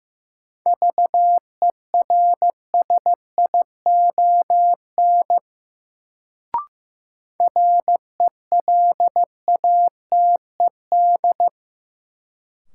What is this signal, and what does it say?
Morse code, heard as a keyed tone, about 15 wpm, 700 Hz. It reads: VERSION RELATED
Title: Sentences from Top 1400 Words - New Word Drill Alt - 15wpm